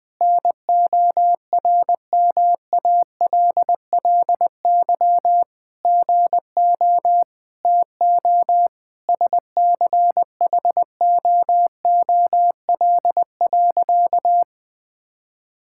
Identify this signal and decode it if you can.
Morse code, heard as a keyed tone, about 20 wpm, 700 Hz. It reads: NORMALLY GO TO SCHOOL.
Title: Sentences from Top 1000 Words - Review Part 6 - 20wpm